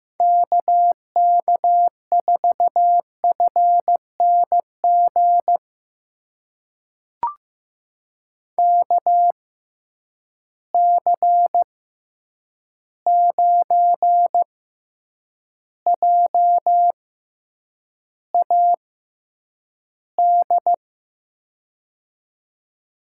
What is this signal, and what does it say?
Morse code, heard as a keyed tone, about 15 wpm, spaced as for 8 wpm, 700 Hz. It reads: KK4FNG KC9JAD
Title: USA Call Signs - 6 Characters - 6x Character Spacing - 15wpm